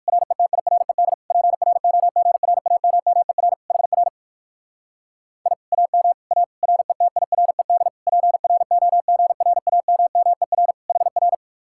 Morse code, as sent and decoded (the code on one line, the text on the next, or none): .-. - .. .-. -.. / .--. .-. --- --. .-. .- -- -- . .-. / .... .-. / .. / .- -- / .- / .-. - .. .-. -.. / .--. .-. --- --. .-. .- -- -- . .-. / .... .-.
RETIRED PROGRAMMER HR I AM A RETIRED PROGRAMMER HR